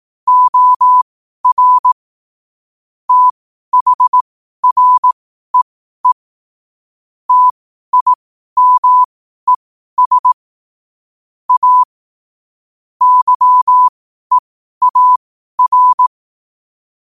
Morse code, heard as a keyed tone, about 18 wpm, spaced as for 12 wpm, 1000 Hz. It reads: OR THREE TIMES A YEAR
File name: qotdfiles / 12wpm